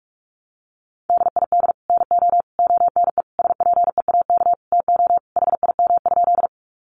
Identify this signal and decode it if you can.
Morse code, heard as a keyed tone, about 44 wpm, 700 Hz. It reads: DID NO ONE SPEAK TO HIM?
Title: Sentences from Top 1100 Words 45wpm